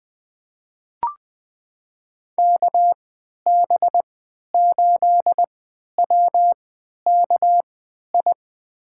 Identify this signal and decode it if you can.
Morse code, heard as a keyed tone, about 20 wpm, spaced as for 11 wpm, 700 Hz. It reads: KB8WKI
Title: USA Call Signs - 6 Characters - 3x Character Spacing - 20wpm